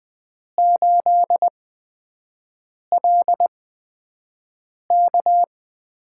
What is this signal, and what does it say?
Morse code, heard as a keyed tone, about 20 wpm, spaced as for 5 wpm, 700 Hz. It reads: 8LK
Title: USA Call Signs - 6 Characters - 8x Character Spacing - 20wpm